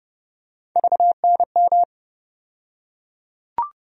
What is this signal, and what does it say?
Morse code, heard as a keyed tone, about 30 wpm, 700 Hz. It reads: VNM
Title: USA Call Sign Postfixes - 3 Letters 30wpm